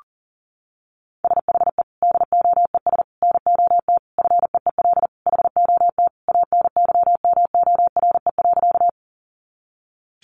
Morse code, heard as a keyed tone, about 40 wpm, 700 Hz. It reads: SHE DOES NOT FEEL HOT ANYMORE.
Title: Additional Easy Phrases 40wpm